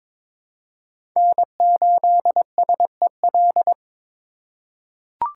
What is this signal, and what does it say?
Morse code, heard as a keyed tone, about 21 wpm, 700 Hz. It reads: N8SEL
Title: USA Call Signs - 5 Characters - 4x Character Spacing - 22wpm